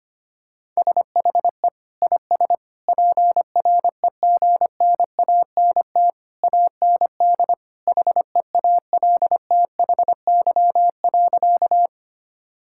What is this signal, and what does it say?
Morse code, heard as a keyed tone, about 25 wpm, 700 Hz. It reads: SHE IS PREGNANT AND HEALTHY.
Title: Sentences from Top 1000 Words - Review Part 6 - 25wpm